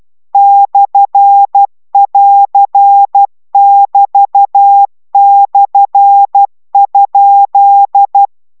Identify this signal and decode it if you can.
Morse code, heard as a keyed tone, about 12 wpm, 800 Hz. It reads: /<AR><BT>/?